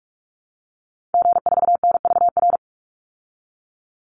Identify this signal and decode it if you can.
Morse code, heard as a keyed tone, about 44 wpm, 700 Hz. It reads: G4NVR
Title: Call Signs Repeated 3x - 45wpm